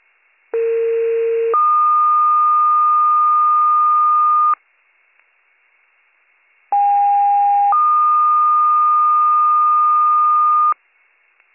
Two-tone page, heard as a one-second tone, then a three-second tone, about 460 then 1220 Hz, about 810 then 1220 Hz. It is heard from 0.5 to 10.7 s.